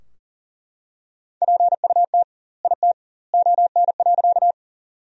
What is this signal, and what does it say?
Morse code, heard as a keyed tone, about 37 wpm, 700 Hz. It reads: PUT IT ON.
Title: 2x Word-Spacing - Sentences from Top 200 Words - 40wpm